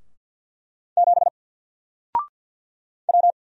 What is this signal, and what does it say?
Morse code, heard as a keyed tone, about 50 wpm, 700 Hz. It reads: 7 W
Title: Single Letter-Number 50wpm